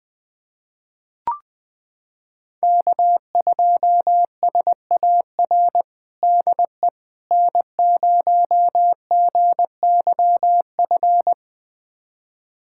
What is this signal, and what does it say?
Morse code, heard as a keyed tone, about 20 wpm, 700 Hz. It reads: K2SAR DE N0GYF
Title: USA Callsign DE Callsign - 5 Characters - 20wpm